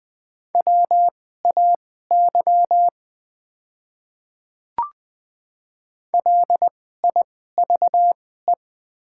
Morse code, heard as a keyed tone, about 20 wpm, spaced as for 14 wpm, 700 Hz. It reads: WAY LIVE
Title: 2x Character Spacing - Top 200 Words - 20wpm